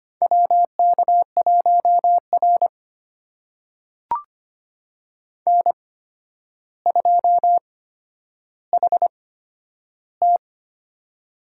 Morse code, heard as a keyed tone, about 25 wpm, spaced as for 11 wpm, 700 Hz. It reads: WK1R N2HT
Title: USA Call Signs - 4 Characters - 8x Character Spacing - 25wpm